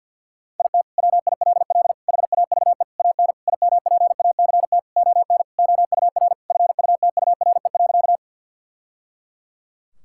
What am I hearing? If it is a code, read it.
Morse code, heard as a keyed tone, about 53 wpm, 700 Hz. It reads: IT WILL HAVE AN IMPACT ON OUR FUTURE.
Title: Sentences from Top 1000 Words - Review Part 6 - 50wpm